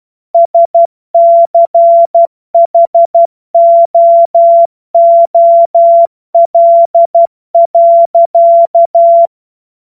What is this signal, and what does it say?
Morse code, heard as a keyed tone, about 12 wpm, 680 Hz. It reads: SCHOOL.
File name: Morsekurs / Witze